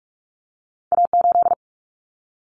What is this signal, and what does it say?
Morse code, heard as a keyed tone, about 46 wpm, 700 Hz. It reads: A8
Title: Common Cut Numbers - 2-Digit 45wpm